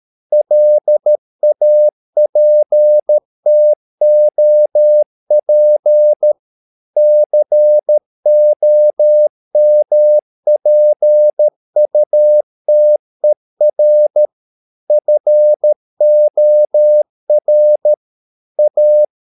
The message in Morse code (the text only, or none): LAPTOP COMPUTER FOR A